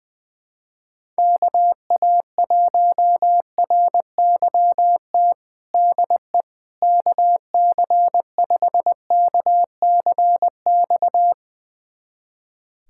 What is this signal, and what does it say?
Morse code, heard as a keyed tone, about 20 wpm, 700 Hz. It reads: KA1RYT DE KC5KCX
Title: USA Callsign DE Callsign - 6 Characters - 20wpm